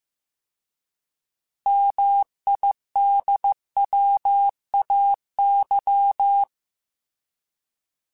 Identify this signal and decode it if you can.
Morse code, heard as a keyed tone, about 15 wpm, 800 Hz. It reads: MIDWAY